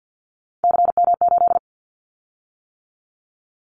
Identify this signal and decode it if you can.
Morse code, heard as a keyed tone, about 50 wpm, 700 Hz. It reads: CM8
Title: Callsign Prefixes for Top 100 DXCC - 50wpm